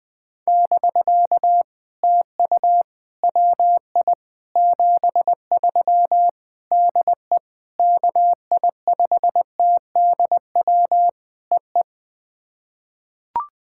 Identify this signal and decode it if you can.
Morse code, heard as a keyed tone, about 20 wpm, 700 Hz. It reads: <BK> TU WI 73 DE KI5TDW EE
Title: POTA QSO Practice - 20wpm